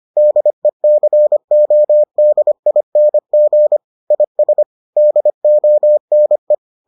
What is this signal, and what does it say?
Morse code, heard as a keyed tone, about 25 wpm, 600 Hz. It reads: DECODING IS DONE